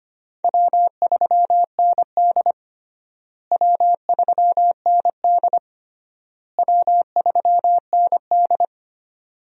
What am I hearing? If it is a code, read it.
Morse code, heard as a keyed tone, about 25 wpm, 700 Hz. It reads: W3ND W3ND W3ND
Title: USA Call Signs - 4 Characters - Repeated 3x - 25wpm